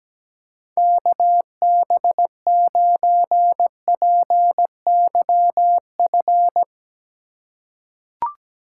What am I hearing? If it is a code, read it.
Morse code, heard as a keyed tone, about 17 wpm, 700 Hz. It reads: KB9PYF
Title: USA Call Signs - 6 Characters - 2x Character Spacing - 17wpm